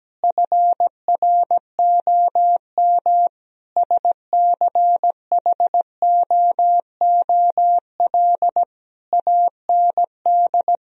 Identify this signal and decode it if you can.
Morse code, heard as a keyed tone, about 17 wpm, 700 Hz. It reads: FROM SCHOOL AND
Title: Sentences from Top 1000 Words - Review Part 4 - 17wpm